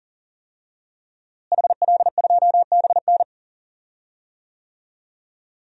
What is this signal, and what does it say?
Morse code, heard as a keyed tone, about 40 wpm, 700 Hz. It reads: HL2BN
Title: DXCC-100 Call Signs - 5 Characters - Repeated 2x - 40wpm